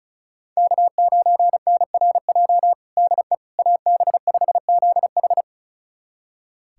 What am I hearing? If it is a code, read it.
Morse code, heard as a keyed tone, about 34 wpm, 700 Hz. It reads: K9NRJ DE AB5ZH
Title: USA Callsign DE Callsign - 5 Characters - 35wpm